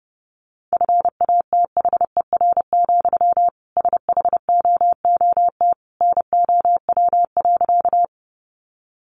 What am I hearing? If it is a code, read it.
Morse code, heard as a keyed tone, about 30 wpm, 700 Hz. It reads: FATHER, SHOOT NOW.
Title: Sentences from Top 1200 Words 30wpm